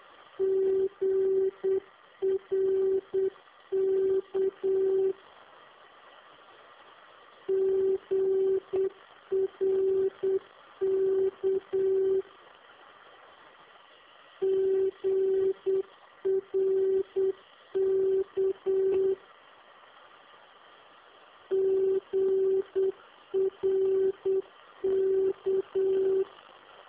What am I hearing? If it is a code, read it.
Morse code, heard as a keyed tone, about 8 wpm, 380 Hz. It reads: GRK GRK GRK GRK